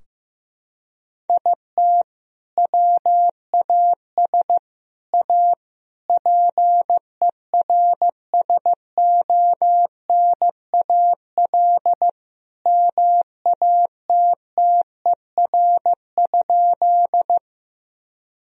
Morse code, heard as a keyed tone, about 15 wpm, 700 Hz. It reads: IT WAS A PERSONAL MATTER?
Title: Sentences from Top 1000 Words - Review Part 4 - 15wpm